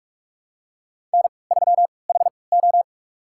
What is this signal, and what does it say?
Morse code, heard as a keyed tone, about 45 wpm, spaced as for 25 wpm, 700 Hz. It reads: N3HO
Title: USA Call Signs - 3x Character Spacing - 45wpm